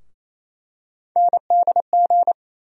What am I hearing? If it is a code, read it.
Morse code, heard as a keyed tone, about 28 wpm, 700 Hz. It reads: NDG
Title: USA Call Sign Postfixes - 3 Letters 28wpm